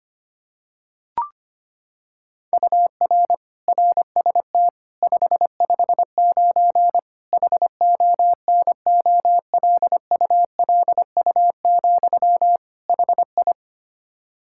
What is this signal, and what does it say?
Morse code, heard as a keyed tone, about 25 wpm, 700 Hz. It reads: UR RST 559 HONOLULU, HI
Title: Practice QSO - Exchange 1 - Beginning - 25wpm